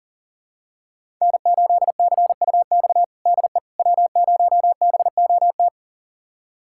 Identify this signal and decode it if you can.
Morse code, heard as a keyed tone, about 40 wpm, 700 Hz. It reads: N8CUX DE W0BOT
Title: USA Callsign DE Callsign - 5 Characters - 40wpm